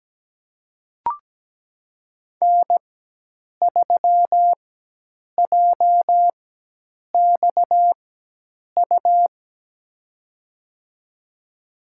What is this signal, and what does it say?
Morse code, heard as a keyed tone, about 17 wpm, spaced as for 8 wpm, 700 Hz. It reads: N3JXU
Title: USA Call Signs - 5 Characters - 4x Character Spacing - 17wpm